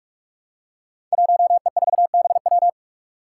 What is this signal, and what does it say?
Morse code, heard as a keyed tone, about 44 wpm, 700 Hz. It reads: 1E4BW
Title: Five Challenging Letter-Number Combinations 45wpm